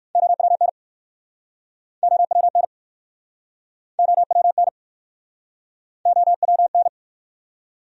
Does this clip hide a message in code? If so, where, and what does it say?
Yes: Morse code from 0.2 to 6.9 s, heard as a keyed tone, about 61 wpm, 700 Hz. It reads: OWN OWN OWN OWN